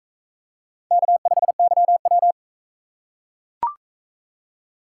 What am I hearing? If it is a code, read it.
Morse code, heard as a keyed tone, about 38 wpm, 700 Hz. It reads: K5YW